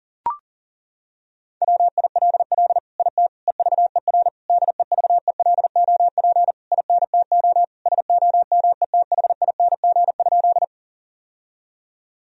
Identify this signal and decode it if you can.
Morse code, heard as a keyed tone, about 40 wpm, 700 Hz. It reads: WILL IT EVER DEVELOP INTO SOMETHING?
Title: Sentences from Top 1000 Words - Review Part 2 - 40wpm